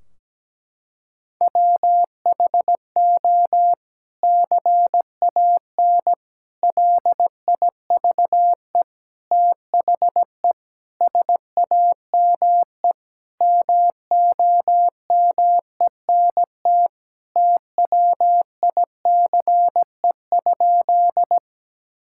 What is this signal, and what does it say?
Morse code, heard as a keyed tone, about 17 wpm, 700 Hz. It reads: WHO CAN LIVE THE SAME MOMENT TWICE?
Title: Sentences from Top 1600 Words 17wpm